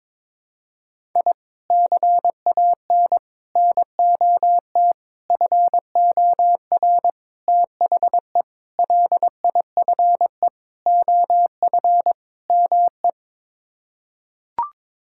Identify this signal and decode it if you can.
Morse code, heard as a keyed tone, about 22 wpm, 700 Hz. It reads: I CAN NOT FOR THE LIFE OF ME
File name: practice / 22wpm / Sets of 8 Words - 22wpm